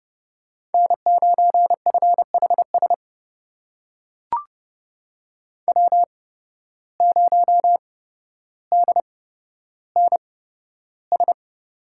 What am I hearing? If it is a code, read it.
Morse code, heard as a keyed tone, about 29 wpm, spaced as for 13 wpm, 700 Hz. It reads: N9FHS W0DNS
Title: USA Call Signs - 5 Characters - 8x Character Spacing - 30wpm